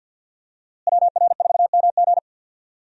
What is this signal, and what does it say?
Morse code, heard as a keyed tone, about 51 wpm, 700 Hz. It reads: WR4MG
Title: USA Call Signs - 5 Characters - Repeated 3x - 50wpm